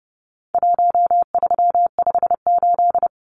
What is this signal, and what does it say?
Morse code, heard as a keyed tone, about 30 wpm, 700 Hz. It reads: 1358
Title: Numbers - Contest Style - 4-Digit 30wpm